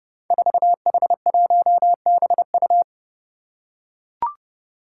Morse code, heard as a keyed tone, about 31 wpm, 700 Hz. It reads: 4H1BU
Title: Five Challenging Letter-Number Combinations 30wpm